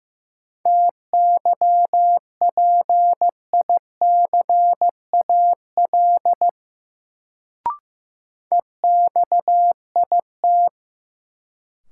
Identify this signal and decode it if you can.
Morse code, heard as a keyed tone, about 15 wpm, 700 Hz. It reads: TYPICAL EXIT
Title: Sentences from Top 1100 Words - New Word Drill - 15wpm